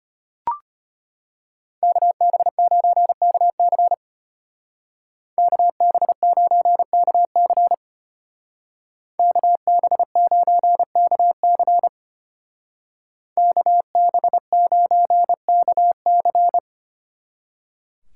Morse code, heard as a keyed tone, about 30 wpm, 700 Hz. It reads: KB9KC KB9KC KB9KC KB9KC
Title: Speed Racing - USA 2x2 Call Signs 25wpm